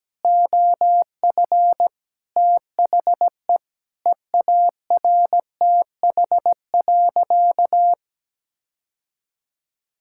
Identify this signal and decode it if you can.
Morse code, heard as a keyed tone, about 17 wpm, 700 Hz. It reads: OF THE EARTH.
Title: Sentences from Top 1600 Words 17wpm